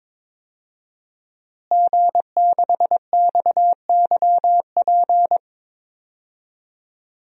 Morse code, heard as a keyed tone, about 22 wpm, 700 Hz. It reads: G6XYP
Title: DXCC-100 Call Signs - 5 Characters - Repeated 3x - 22wpm